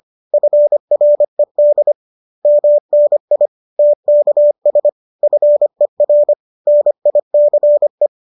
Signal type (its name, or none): Morse code